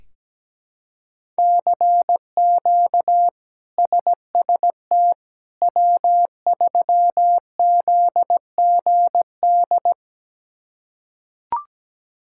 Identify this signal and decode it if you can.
Morse code, heard as a keyed tone, about 17 wpm, 700 Hz. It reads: CQ SST W3ZGD
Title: K1USN SST QSO Practice - 17wpm